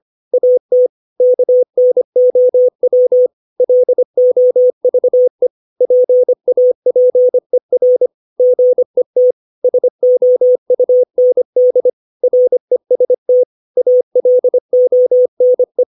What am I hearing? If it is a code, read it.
Morse code, heard as a keyed tone, about 25 wpm, 500 Hz. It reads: AT KNOW LOVE PAPER GET SOUND REST ALONE
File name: 632 words / 632 Words 25WPM